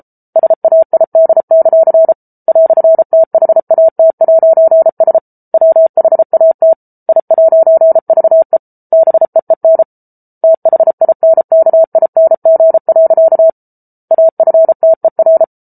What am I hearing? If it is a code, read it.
Morse code, heard as a keyed tone, about 34 wpm, 650 Hz. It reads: SAID; "THAT'S WHAT I'VE BEEN THINKING. AFTER